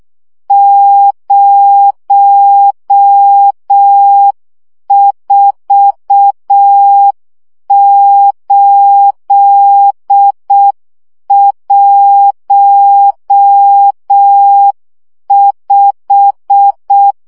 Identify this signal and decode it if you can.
Morse code, heard as a keyed tone, about 6 wpm, 800 Hz. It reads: 04815
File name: morze vizsga 2011-1 5